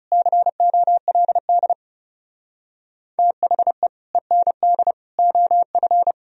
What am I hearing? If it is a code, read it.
Morse code, heard as a keyed tone, about 39 wpm, 700 Hz. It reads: COLD THE END OF